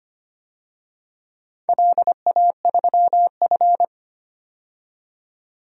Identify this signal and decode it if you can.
Morse code, heard as a keyed tone, about 25 wpm, 700 Hz. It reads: LA3F
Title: DXCC-100 Call Signs - 4 Characters - 25wpm